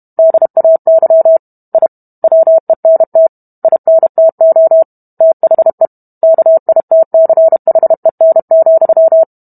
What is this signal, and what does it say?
Morse code, heard as a keyed tone, about 32 wpm, 650 Hz. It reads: DAY I WENT INTO THE KITCHEN,